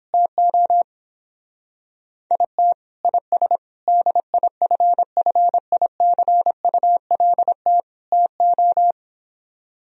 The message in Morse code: TO IT IS DIFFICULT TO

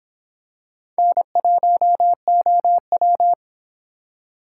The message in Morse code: N1OW

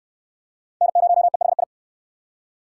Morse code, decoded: N0EHI

